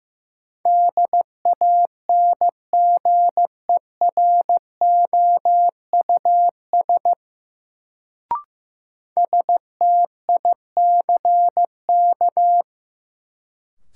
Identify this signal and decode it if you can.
Morse code, heard as a keyed tone, about 15 wpm, 700 Hz. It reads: DANGEROUS STICK
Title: Sentences from Top 1200 Words - New Word Drill Alt - 15wpm